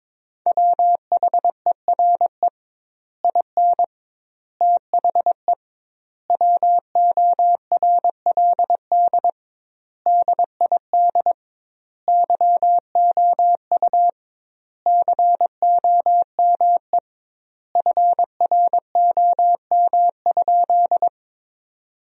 Morse code, heard as a keyed tone, about 22 wpm, 700 Hz. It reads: WHERE IN THE WORLD DID YOU COME FROM?
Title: 2x Word-Spacing - Sentences from Top 200 Words - 22wpm